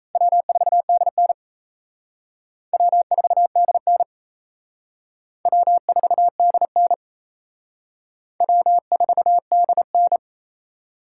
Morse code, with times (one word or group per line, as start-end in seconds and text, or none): 0.1-1.3 W4DN
2.7-4.0 W4DN
5.5-6.9 W4DN
8.4-10.2 W4DN